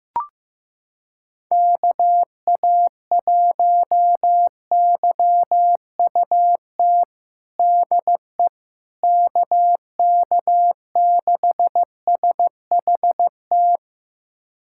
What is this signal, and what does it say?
Morse code, heard as a keyed tone, about 15 wpm, 700 Hz. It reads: KA1YUT DE KK6SHT
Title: USA Callsign DE Callsign 15wpm